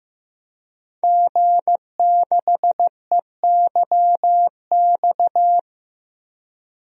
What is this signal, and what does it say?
Morse code, heard as a keyed tone, about 15 wpm, 700 Hz. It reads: G6EYX
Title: DXCC-100 Call Signs - 5 Characters - Repeated 3x - 15wpm